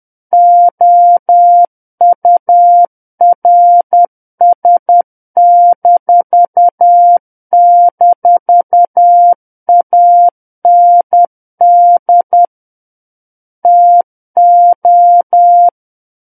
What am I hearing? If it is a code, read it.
Morse code, heard as a keyed tone, about 10 wpm, 700 Hz. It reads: OURS--AND TO
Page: War of the Worlds - 01-Chapter 1 - 10 WPM